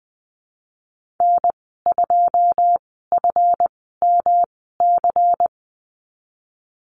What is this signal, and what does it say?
Morse code, heard as a keyed tone, about 20 wpm, spaced as for 15 wpm, 700 Hz. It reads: N2FMC
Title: 173 Lesson 28g - Number 9 - Call Signs 20wpm